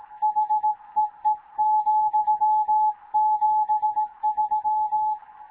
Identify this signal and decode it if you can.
Morse code, heard as a keyed tone, about 18 wpm, 860 Hz. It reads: HEE,73
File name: Morze-HA2MN / hangp / 2 11 GYAKORLAT